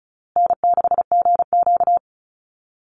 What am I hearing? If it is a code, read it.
Morse code, heard as a keyed tone, about 34 wpm, 700 Hz. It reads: N6GQ
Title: Easy Call Signs Repeated 3x - 35wpm